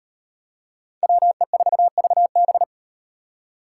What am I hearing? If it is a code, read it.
Morse code, heard as a keyed tone, about 38 wpm, 700 Hz. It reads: WE4VB